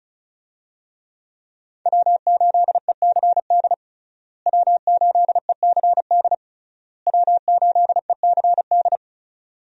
Morse code, heard as a keyed tone, about 36 wpm, 700 Hz. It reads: W8ECD W8ECD W8ECD
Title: USA Call Signs - 5 Characters - Repeated 3x - 35wpm